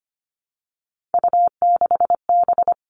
Morse code, heard as a keyed tone, about 25 wpm, 700 Hz. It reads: U6B
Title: All Cut Numbers - 3-Digit 25wpm